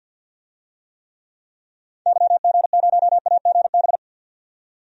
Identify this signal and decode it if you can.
Morse code, heard as a keyed tone, about 47 wpm, 700 Hz. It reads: YG0AGB
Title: DXCC-100 Call Signs - 6 Characters - 50wpm